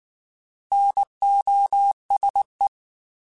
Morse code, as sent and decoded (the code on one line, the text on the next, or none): -. --- ... .
NOSE